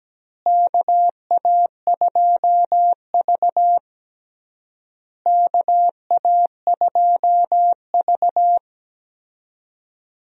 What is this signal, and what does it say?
Morse code, heard as a keyed tone, about 17 wpm, 700 Hz. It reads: KA2V KA2V
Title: USA Call Signs - 4 Characters - Repeated 3x - 17wpm